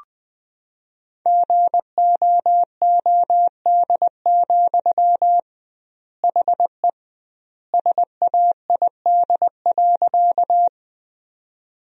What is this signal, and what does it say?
Morse code, heard as a keyed tone, about 20 wpm, 700 Hz. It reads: GOOD, HE SAID.
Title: 2x Word-Spacing - Sentences from Top 200 Words - 20wpm